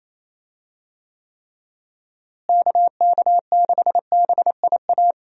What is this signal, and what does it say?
Morse code, heard as a keyed tone, about 27 wpm, 700 Hz. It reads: KK6BIA
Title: USA Call Signs - 6 Characters - Repeated 3x - 28wpm